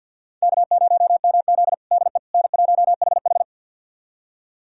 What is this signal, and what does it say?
Morse code, heard as a keyed tone, about 51 wpm, 700 Hz. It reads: K0MZ DE N1HH
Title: USA Callsign DE Callsign - 4 Characters - 50wpm